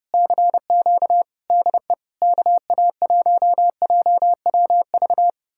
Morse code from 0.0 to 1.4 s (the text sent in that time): CQ